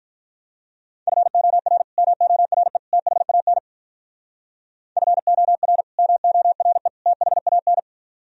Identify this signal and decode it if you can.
Morse code, heard as a keyed tone, about 46 wpm, 700 Hz. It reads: FOR MORE THAN FOR MORE THAN